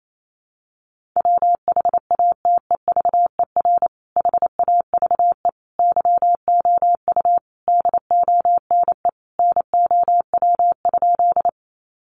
Morse code, heard as a keyed tone, about 28 wpm, 700 Hz. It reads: WHATEVER HAVE YOU DONE NOW?
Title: Sentences from Top 1100 Words 28wpm